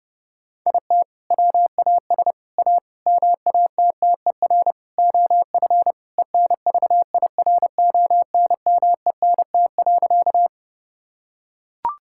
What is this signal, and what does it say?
Morse code, heard as a keyed tone, about 30 wpm, 700 Hz. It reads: IT WAS A MATTER OF ENVIRONMENT.
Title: Sentences from Top 1000 Words - Review Part 5 - 30wpm